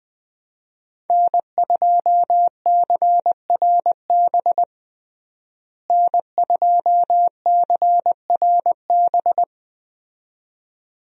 Morse code, heard as a keyed tone, about 20 wpm, 700 Hz. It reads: N2CRB N2CRB
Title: USA Call Signs - 5 Characters - Repeated 2x - 20wpm